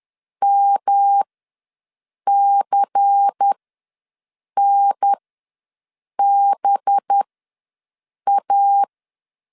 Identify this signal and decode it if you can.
Morse code, heard as a keyed tone, about 10 wpm, spaced as for 6 wpm, 800 Hz. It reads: MCNBA
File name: LEZIONI / 6 PKX0 / 6 PKX ws 10wpm 3706